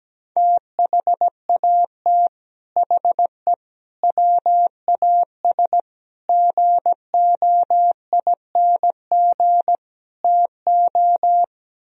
Morse code, heard as a keyed tone, about 17 wpm, 700 Hz. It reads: THAT HE WAS GOING TO
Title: Speed-Racing - Sets of 5 Words 17wpm